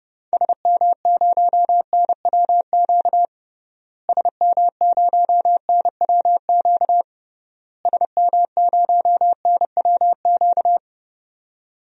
Morse code, heard as a keyed tone, about 32 wpm, 700 Hz. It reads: SM0NWQ SM0NWQ SM0NWQ
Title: DXCC-100 Call Signs - 6 Characters - Repeated 3x - 30wpm